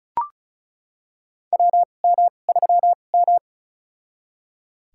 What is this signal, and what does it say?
Morse code, heard as a keyed tone, about 36 wpm, spaced as for 25 wpm, 700 Hz. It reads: WM3M
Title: USA Call Signs - 2x Character Spacing - 35wpm